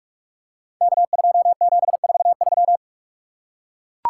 Morse code, heard as a keyed tone, about 44 wpm, 700 Hz. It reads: K2743
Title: Code Groups - 5 Letter:Numbers - 45wpm